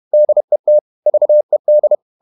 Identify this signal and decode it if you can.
Morse code, heard as a keyed tone, about 31 wpm, 600 Hz. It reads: DET VED
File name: Ferie QRQ 31wpm